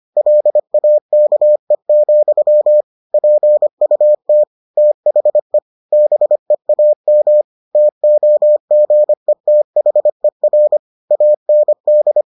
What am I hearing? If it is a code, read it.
Morse code, heard as a keyed tone, about 25 wpm, 600 Hz. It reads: LAKE, PUT THE BEAM TOGETHER AND